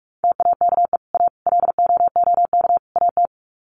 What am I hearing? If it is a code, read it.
Morse code, heard as a keyed tone, about 44 wpm, 700 Hz. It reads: TAKE A LOOK AT
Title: Sets of 4 Words 45wpm